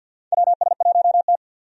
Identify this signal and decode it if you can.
Morse code, heard as a keyed tone, about 51 wpm, 700 Hz. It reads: WS1T
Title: USA Call Signs - 4 Characters - 50wpm